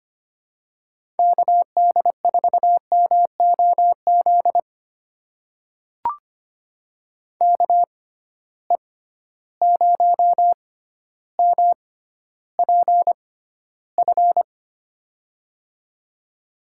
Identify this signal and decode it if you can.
Morse code, heard as a keyed tone, about 25 wpm, spaced as for 13 wpm, 700 Hz. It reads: KD4MOZ KE0MPF
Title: USA Call Signs - 6 Characters - 6x Character Spacing - 25wpm